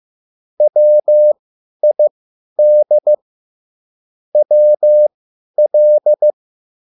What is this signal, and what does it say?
Morse code, heard as a keyed tone, about 15 wpm, spaced as for 10 wpm, 600 Hz. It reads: WID WL